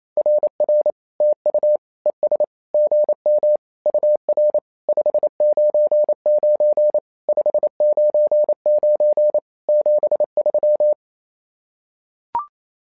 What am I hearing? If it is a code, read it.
Morse code, heard as a keyed tone, about 27 wpm, 600 Hz. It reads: RR TU ES GM UR 599 599 73
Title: SOTA QSO Practice 28wpm